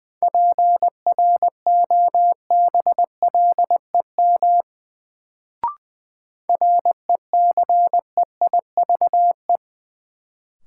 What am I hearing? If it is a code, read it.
Morse code, heard as a keyed tone, about 20 wpm, 700 Hz. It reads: PROBLEM RECEIVE
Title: Sentences from Top 1000 Words - New Word Drill - 20wpm